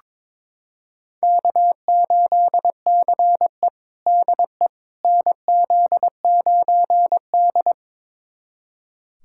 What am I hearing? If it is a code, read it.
Morse code, heard as a keyed tone, about 22 wpm, 700 Hz. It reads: K8CE DE NZ9D
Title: USA Callsign DE Callsign - 4 Characters - 22wpm